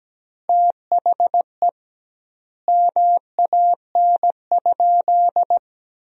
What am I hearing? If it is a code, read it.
Morse code, heard as a keyed tone, about 17 wpm, 700 Hz. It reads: THE MAN?
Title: 2x Word-Spacing - Sentences from Top 200 Words - 17wpm